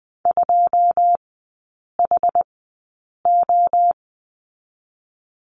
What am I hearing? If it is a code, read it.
Morse code, heard as a keyed tone, about 20 wpm, spaced as for 8 wpm, 690 Hz. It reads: 2HO
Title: 145 Lesson 24e - Letter y - Three Characters 20wpm